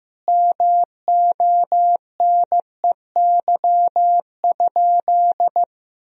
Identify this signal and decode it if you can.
Morse code, heard as a keyed tone, about 15 wpm, 700 Hz. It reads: MONEY?
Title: Sentences from Top 400 Words and Abbreviations 15wpm